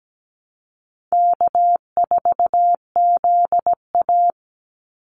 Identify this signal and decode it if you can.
Morse code, heard as a keyed tone, about 17 wpm, 700 Hz. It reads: K4ZA
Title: Easy Call Signs - 17wpm